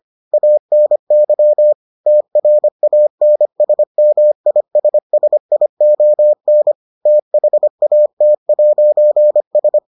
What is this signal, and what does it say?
Morse code, heard as a keyed tone, about 25 wpm, 600 Hz. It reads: ANY TRANSMISSION THAT'S